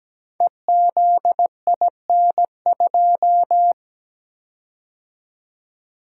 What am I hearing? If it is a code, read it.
Morse code, heard as a keyed tone, about 17 wpm, 700 Hz. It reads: EZIN2
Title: Code Groups - 5 Letter:Numbers - 17wpm